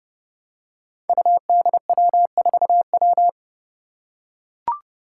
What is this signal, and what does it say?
Morse code, heard as a keyed tone, about 29 wpm, 700 Hz. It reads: UDW4W
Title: Five Challenging Letter-Number Combinations 30wpm